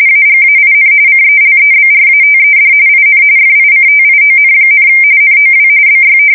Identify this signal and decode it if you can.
RTTY, about 45.45 baud. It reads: AGAIN CELEBRATED IN FICTION, NOW BY
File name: fsk170 / battleshipnj